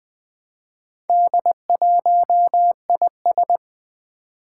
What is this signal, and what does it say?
Morse code, heard as a keyed tone, about 20 wpm, 700 Hz. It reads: D1IS
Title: Four Challenging Letter-Number Combinations 20wpm